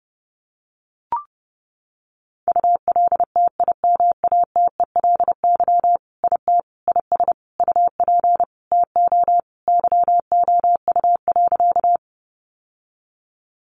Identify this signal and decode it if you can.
Morse code, heard as a keyed tone, about 30 wpm, 700 Hz. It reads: ULTIMATELY IT IS UP TO YOU.
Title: Sentences from Top 1200 Words 30wpm